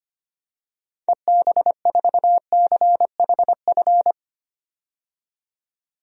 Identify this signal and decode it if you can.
Morse code, heard as a keyed tone, about 25 wpm, 700 Hz. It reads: EB4CHF
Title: DXCC-100 Call Signs - 6 Characters - 25wpm